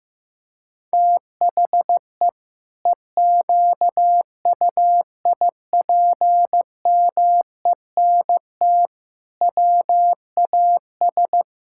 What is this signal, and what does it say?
Morse code, heard as a keyed tone, about 15 wpm, 700 Hz. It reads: THE EQUIPMENT WAS
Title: Sentences from Top 1000 Words - Review Part 5 - 15wpm